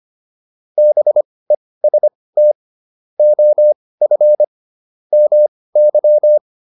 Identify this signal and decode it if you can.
Morse code, heard as a keyed tone, about 25 wpm, spaced as for 18 wpm, 600 Hz. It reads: BEST OF MY